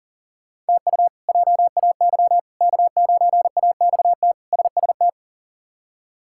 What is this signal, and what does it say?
Morse code, heard as a keyed tone, about 40 wpm, 700 Hz. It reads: TU JAY K9AXT SST
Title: K1USN SST QSO Practice - 40wpm